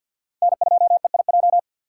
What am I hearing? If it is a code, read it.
Morse code, heard as a keyed tone, about 53 wpm, 700 Hz. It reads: N2EIJ
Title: USA Call Signs - 3x Character Spacing - 50wpm